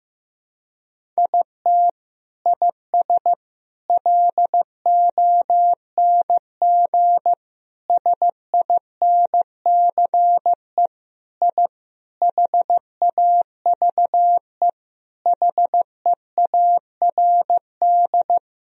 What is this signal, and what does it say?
Morse code, heard as a keyed tone, about 15 wpm, 700 Hz. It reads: IT IS LONG SINCE I HAVE HEARD
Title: Sentences from Top 1000 Words - Review Part 1 - 15wpm